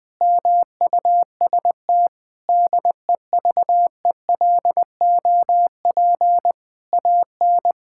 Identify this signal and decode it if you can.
Morse code, heard as a keyed tone, about 20 wpm, 700 Hz. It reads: MUST DEVELOP AN
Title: Sentences from Top 1000 Words - Review Part 5 - 20wpm